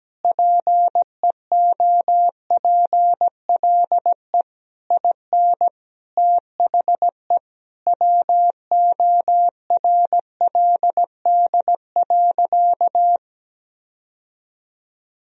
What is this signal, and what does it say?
Morse code, heard as a keyed tone, about 17 wpm, 700 Hz. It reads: PEOPLE IN THE WORLD.
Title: Sentences from Top 1000 Words - Review Part 4 - 17wpm